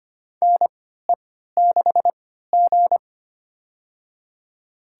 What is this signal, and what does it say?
Morse code, heard as a keyed tone, about 25 wpm, spaced as for 14 wpm, 700 Hz. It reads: NE6G
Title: USA Call Signs - 4 Characters - 3x Character Spacing - 25wpm